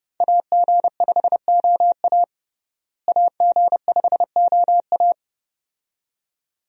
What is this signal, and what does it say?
Morse code, heard as a keyed tone, about 32 wpm, 700 Hz. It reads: AG5OA AG5OA
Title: USA Call Signs - 5 Characters - Repeated 2x - 30wpm